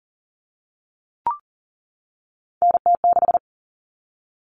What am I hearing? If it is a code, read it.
Morse code, heard as a keyed tone, about 40 wpm, 700 Hz. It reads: NT6
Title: All Cut Numbers - 3-Digit 40wpm